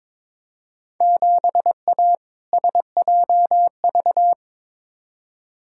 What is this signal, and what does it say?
Morse code, heard as a keyed tone, about 22 wpm, 700 Hz. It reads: 7A SJV
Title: Field Day Exchanges Encore - 22wpm